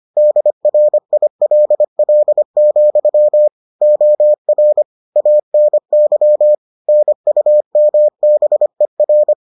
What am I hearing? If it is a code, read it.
Morse code, heard as a keyed tone, about 25 wpm, 600 Hz. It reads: DRILL, OR ANY NUMBER